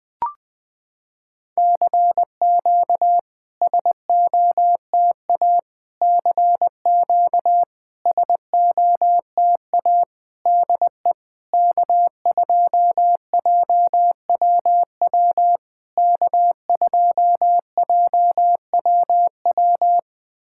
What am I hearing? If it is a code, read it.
Morse code, heard as a keyed tone, about 20 wpm, 700 Hz. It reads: CQ SOTA CQ SOTA DE K2JWW K2JWW
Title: SOTA QSO Practice 20wpm